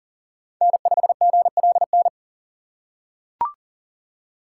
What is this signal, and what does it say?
Morse code, heard as a keyed tone, about 38 wpm, 700 Hz. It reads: N5GLN
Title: USA Call Signs - 5 Characters - 2x Character Spacing - 40wpm